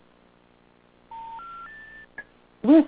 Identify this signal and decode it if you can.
Special information tone.